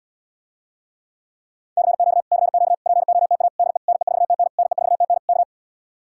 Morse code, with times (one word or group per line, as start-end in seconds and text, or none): 1.8-2.2 CQ
2.3-2.7 CQ
2.9-3.5 POTA
3.6-3.8 DE
3.9-4.5 NE4TN
4.6-5.2 NE4TN
5.3-5.4 K